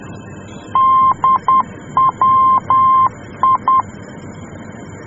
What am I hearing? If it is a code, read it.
Morse code, heard as a keyed tone, about 10 wpm, 1050 Hz. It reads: DWI
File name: germany / mx / DWI-D